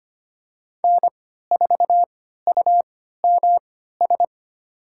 Morse code, heard as a keyed tone, about 25 wpm, spaced as for 14 wpm, 700 Hz. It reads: N4UMS
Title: USA Call Signs - 5 Characters - 3x Character Spacing - 25wpm